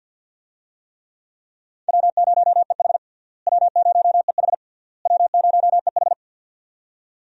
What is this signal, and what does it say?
Morse code, heard as a keyed tone, about 47 wpm, 700 Hz. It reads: W0EH W0EH W0EH
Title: USA Call Signs - 4 Characters - Repeated 3x - 50wpm